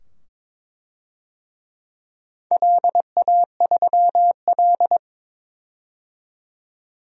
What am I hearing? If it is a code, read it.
Morse code, heard as a keyed tone, about 22 wpm, 700 Hz. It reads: LA3L
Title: DXCC-100 Call Signs - 4 Characters - 22wpm